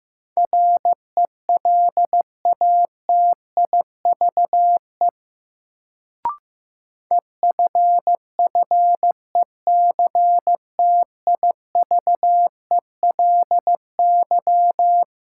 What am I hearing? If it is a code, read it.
Morse code, heard as a keyed tone, about 15 wpm, 700 Hz. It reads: RELATIVE EFFECTIVELY
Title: Sentences from Top 1400 Words - New Word Drill - 15wpm